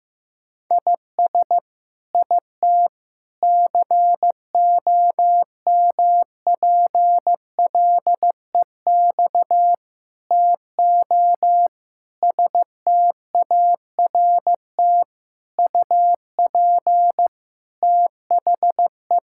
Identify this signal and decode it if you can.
Morse code, heard as a keyed tone, about 16 wpm, 700 Hz. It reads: IS IT COMPLEX TO START UP THE
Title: Sentences from Top 1000 Words - Review Part 6 - 15wpm